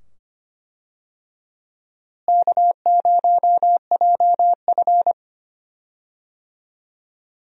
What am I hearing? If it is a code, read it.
Morse code, heard as a keyed tone, about 25 wpm, 700 Hz. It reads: K0JF
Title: USA Call Signs - 4 Characters - Repeated 2x - 25wpm